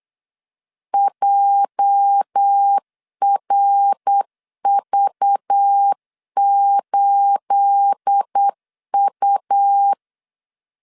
Morse code, heard as a keyed tone, about 8 wpm, 800 Hz. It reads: JRV8U
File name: CW / DET / 1 9 1500C S B 8wpm 5624